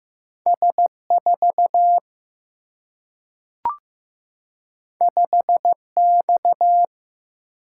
Morse code, heard as a keyed tone, about 15 wpm, 700 Hz. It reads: S4 5X
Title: Single-Letter Single-Number 15wpm